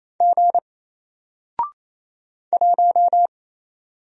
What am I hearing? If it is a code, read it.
Morse code, heard as a keyed tone, about 27 wpm, 700 Hz. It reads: G 1
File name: Single-Letter-Number-28wpm